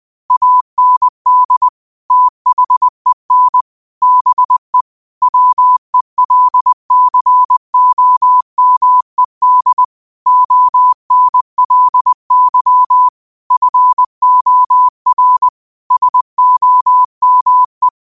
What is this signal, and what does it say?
Morse code, heard as a keyed tone, about 20 wpm, 1000 Hz. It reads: AND THEN BE WELCOMED ONLY FOR SOME
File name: qotdfiles / 20wpm